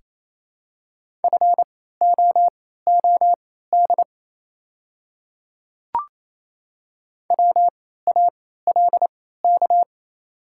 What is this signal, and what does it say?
Morse code, heard as a keyed tone, about 28 wpm, spaced as for 16 wpm, 700 Hz. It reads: FOOD WALK
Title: 3x Character Spacing - Top 300 Words - 28wpm